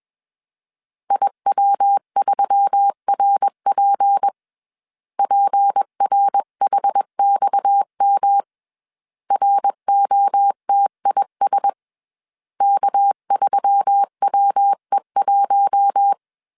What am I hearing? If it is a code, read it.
Morse code, heard as a keyed tone, about 21 wpm, 800 Hz. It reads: IW3RP PRHXM ROTIS K3WE1
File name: CW / DET / 1 6 1500C S C 20wpm 2116